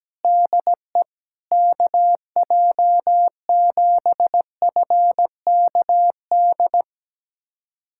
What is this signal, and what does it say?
Morse code, heard as a keyed tone, about 17 wpm, 700 Hz. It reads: DE KJ7FKD
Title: USA Callsign DE Callsign 17wpm